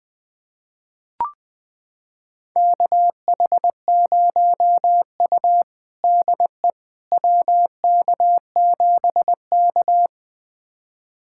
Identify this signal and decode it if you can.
Morse code, heard as a keyed tone, about 21 wpm, 700 Hz. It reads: KH0U DE WK7K
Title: USA Callsign DE Callsign - 4 Characters - 20wpm